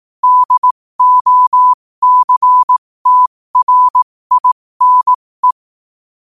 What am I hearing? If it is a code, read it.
Morse code, heard as a keyed tone, about 18 wpm, spaced as for 15 wpm, 1000 Hz. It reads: DOCTRINE
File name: qotdfiles / 15wpm